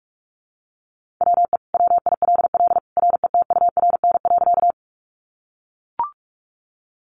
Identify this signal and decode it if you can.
Morse code, heard as a keyed tone, about 44 wpm, 700 Hz. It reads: WE WILL RETURN.
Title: Sentences from Top 800 Words 45wpm